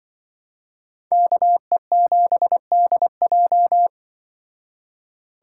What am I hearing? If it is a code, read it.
Morse code, heard as a keyed tone, about 23 wpm, 700 Hz. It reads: KE7DJ
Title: Speed Racing - USA 2x2 Call Signs 20wpm